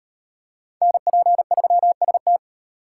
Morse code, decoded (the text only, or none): NP3ST